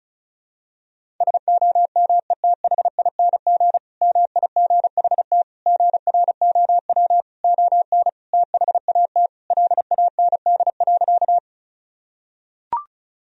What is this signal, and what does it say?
Morse code, heard as a keyed tone, about 34 wpm, 700 Hz. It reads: SOMETHING MIGHT GROW ON THAT LAND.
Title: Sentences from Top 1000 Words - Review Part 1 - 35wpm